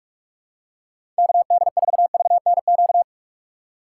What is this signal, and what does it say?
Morse code, heard as a keyed tone, about 46 wpm, 700 Hz. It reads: KD4VNQ
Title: USA Call Signs - 6 Characters - 6x Character Spacing - 45wpm